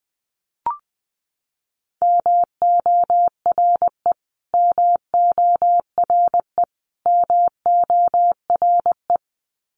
Morse code, heard as a keyed tone, about 20 wpm, 700 Hz. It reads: MORE MORE MORE
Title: Top 300 Words Repeated 20wpm